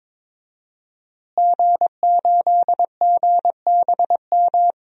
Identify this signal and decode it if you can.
Morse code, heard as a keyed tone, about 22 wpm, 700 Hz. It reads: G8GBM
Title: DXCC-100 Call Signs - 5 Characters - Repeated 2x - 22wpm